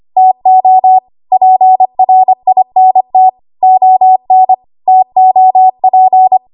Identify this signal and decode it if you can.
Morse code, heard as a keyed tone, about 25 wpm, 750 Hz. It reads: TO PRINT ON TOP
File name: Archive / 25 WPM / 25WPM